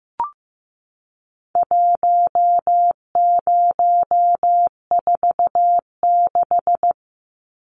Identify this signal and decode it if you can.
Morse code, heard as a keyed tone, about 15 wpm, 700 Hz. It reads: 1046
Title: Numbers - 4-Digit 15wpm